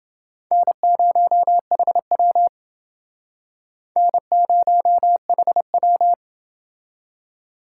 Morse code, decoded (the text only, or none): N0HW N0HW